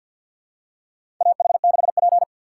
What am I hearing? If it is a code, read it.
Morse code, heard as a keyed tone, about 50 wpm, 700 Hz. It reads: AH6P
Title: USA Call Signs - 4 Characters - 6x Character Spacing - 50wpm